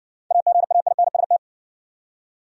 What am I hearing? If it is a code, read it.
Morse code, heard as a keyed tone, about 58 wpm, 700 Hz. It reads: AGAINST